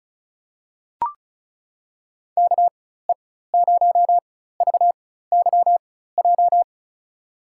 Morse code, heard as a keyed tone, about 34 wpm, spaced as for 16 wpm, 700 Hz. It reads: KE0VYJ
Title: USA Call Signs - 6 Characters - 4x Character Spacing - 35wpm